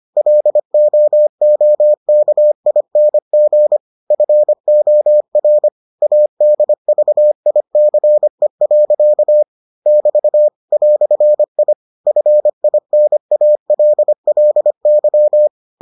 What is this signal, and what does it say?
Morse code, heard as a keyed tone, about 25 wpm, 600 Hz. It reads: LOOKING FOR ADVICE. <BT> "I FINALLY